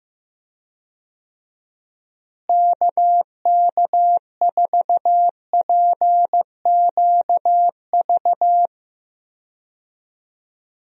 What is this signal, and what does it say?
Morse code, heard as a keyed tone, about 15 wpm, 700 Hz. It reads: KK4PQV
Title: USA Call Signs - 6 Characters - Repeated 2x - 15wpm